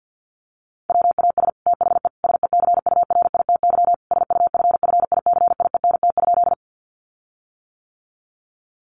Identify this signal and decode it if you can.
Morse code, heard as a keyed tone, about 50 wpm, 700 Hz. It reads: WAS THE SECURITY SUFFICIENT?
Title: Sentences from Top 1400 Words 50wpm